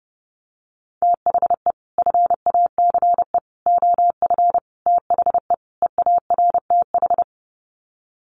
Morse code, heard as a keyed tone, about 30 wpm, 700 Hz. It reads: THE FACE OF THE EARTH
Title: Sets of 5 Words Encore 30wpm